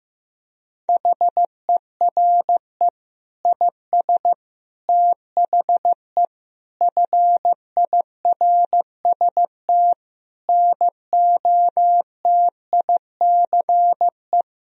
Morse code, heard as a keyed tone, about 15 wpm, 700 Hz. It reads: HERE IS THE FIRST NOTICE
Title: Sentences from Top 1000 Words - Review Part 2 - 15wpm